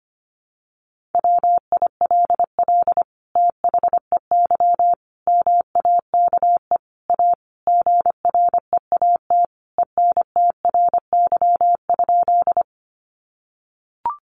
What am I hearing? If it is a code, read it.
Morse code, heard as a keyed tone, about 25 wpm, 700 Hz. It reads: WILL THEY MAKE A GREAT ENTRY?
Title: Sentences from Top 1500 Words 25wpm